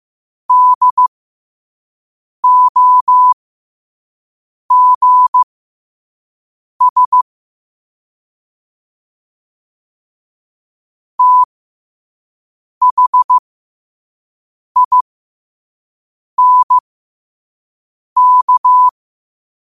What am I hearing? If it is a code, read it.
Morse code, heard as a keyed tone, about 15 wpm, spaced as for 5 wpm, 1000 Hz. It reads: DOGS THINK
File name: qotdfiles / 5wpm